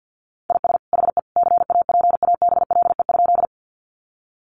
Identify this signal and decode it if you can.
Morse code, heard as a keyed tone, about 50 wpm, 700 Hz. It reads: IS HE CAPABLE?
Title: Sentences from Top 1200 Words 50wpm